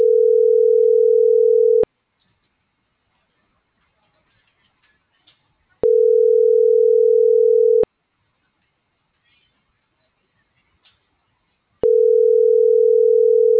Ringback tone (North America), 440 + 480 Hz.